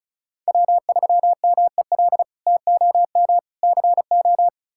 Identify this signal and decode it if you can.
Morse code, heard as a keyed tone, about 34 wpm, 700 Hz. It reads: W3MEL TOM CO
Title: K1USN SST QSO Practice - 35wpm